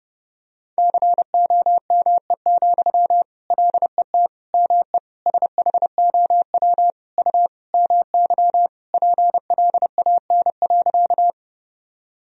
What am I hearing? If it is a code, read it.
Morse code, heard as a keyed tone, about 30 wpm, 700 Hz. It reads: COME, LET ME SHOW U MY PLAN.
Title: Sentences from Top 500 Words and Abbreviations 30wpm